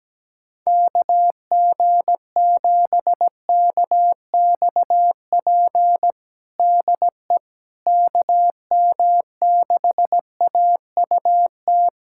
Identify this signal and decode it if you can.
Morse code, heard as a keyed tone, about 17 wpm, 700 Hz. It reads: KG7KXP DE KM6AUT
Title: USA Callsign DE Callsign - 6 Characters - 17wpm